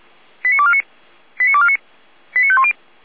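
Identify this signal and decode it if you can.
Five-tone selective call (ZVEI), heard as quick runs of five tones, tones of about 70 ms. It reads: 89249 89249 89510
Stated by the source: ZVEI five-tone call received by SDR, archived by Feuerwehr Orlamünde.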